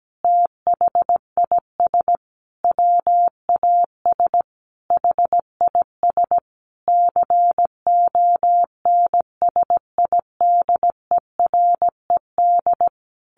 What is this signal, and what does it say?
Morse code, heard as a keyed tone, about 17 wpm, 700 Hz. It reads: THIS WAS HIS CONSIDERED
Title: A Sound Of Thunder 17wpm